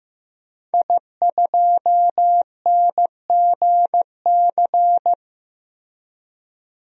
Morse code, heard as a keyed tone, about 15 wpm, 700 Hz. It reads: I2NGC
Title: DXCC-100 Call Signs - 5 Characters - Repeated 3x - 15wpm